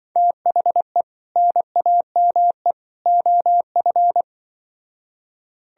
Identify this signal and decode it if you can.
Morse code, heard as a keyed tone, about 25 wpm, 700 Hz. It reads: THE NAME OF